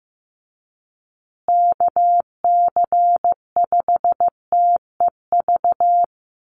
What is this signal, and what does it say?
Morse code, heard as a keyed tone, about 15 wpm, 700 Hz. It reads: KC5TEV
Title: USA Call Signs Repeated 2x 15wpm